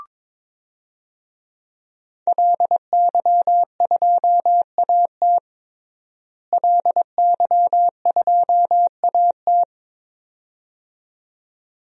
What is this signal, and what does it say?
Morse code, heard as a keyed tone, about 22 wpm, 700 Hz. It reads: LY2AT LY2AT
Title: DXCC-100 Call Signs - 5 Characters - Repeated 2x - 22wpm